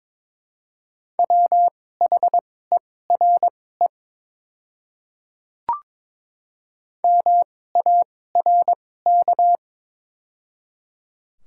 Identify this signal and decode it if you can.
Morse code, heard as a keyed tone, about 22 wpm, spaced as for 16 wpm, 700 Hz. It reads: WHERE MARK
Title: 2x Character Spacing - Top 300 Words - 22wpm